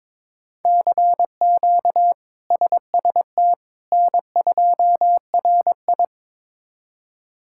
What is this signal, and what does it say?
Morse code, heard as a keyed tone, about 22 wpm, 700 Hz. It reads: CQ SST N2RI
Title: K1USN SST QSO Practice - 22wpm